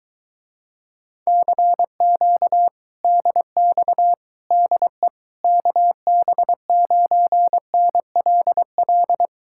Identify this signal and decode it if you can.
Morse code, heard as a keyed tone, about 24 wpm, 700 Hz. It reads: CQ DX DE KB9NLL